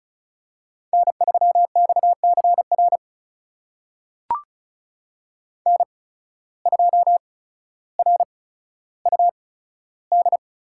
Morse code, heard as a keyed tone, about 36 wpm, spaced as for 15 wpm, 700 Hz. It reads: N3XCR N2RUD
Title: USA Call Signs - 5 Characters - 8x Character Spacing - 35wpm